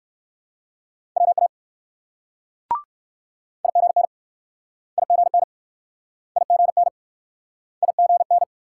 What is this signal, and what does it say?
Morse code, heard as a keyed tone, about 64 wpm, 700 Hz. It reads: WA IGN IGN IGN IGN